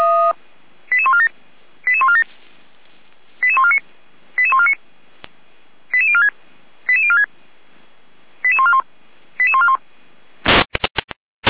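Five-tone selective call (ZVEI), heard as quick runs of five tones, tones of about 70 ms. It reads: 88147 88147 88149 88149 88046 88046 88141 88141
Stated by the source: ZVEI five-tone selective call 88397 received on an SDR.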